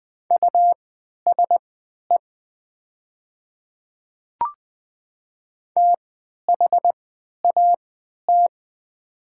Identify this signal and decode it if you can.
Morse code, heard as a keyed tone, about 20 wpm, spaced as for 11 wpm, 700 Hz. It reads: USE THAT
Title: 3x Character Spacing - Top 100 Words - 20wpm